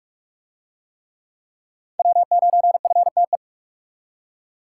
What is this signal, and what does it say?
Morse code, heard as a keyed tone, about 41 wpm, 700 Hz. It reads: W9FTE